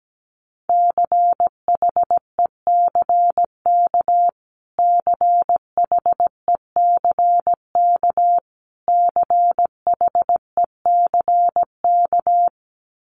Morse code, heard as a keyed tone, about 17 wpm, 700 Hz. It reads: CHECK CHECK CHECK
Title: Top 500 Words Repeated 17wpm